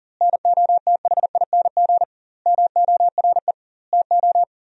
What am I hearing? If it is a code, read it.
Morse code, heard as a keyed tone, about 40 wpm, 700 Hz. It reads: NOTHING MORE TO
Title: 2x Word-Spacing - Sentences from Top 500 Words - 40wpm